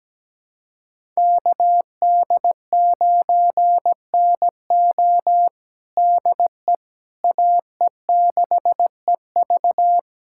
Morse code, heard as a keyed tone, about 17 wpm, 700 Hz. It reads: KD9NO DE AE6EV
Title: USA Callsign DE Callsign - 5 Characters - 17wpm